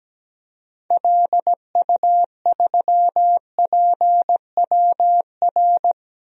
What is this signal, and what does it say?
Morse code, heard as a keyed tone, about 17 wpm, 700 Hz. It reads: LU3PWR
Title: DXCC-100 Call Signs - 6 Characters - 17wpm